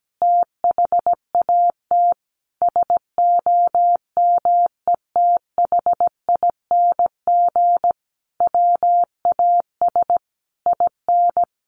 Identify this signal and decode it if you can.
Morse code, heard as a keyed tone, about 17 wpm, 700 Hz. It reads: THAT SOMETHING WAS IN
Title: Sentences from Top 1400 Words 17wpm